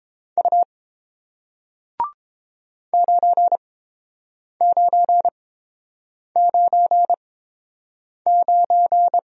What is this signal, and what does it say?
Morse code, heard as a keyed tone, about 30 wpm, spaced as for 7 wpm, 700 Hz. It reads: U 9999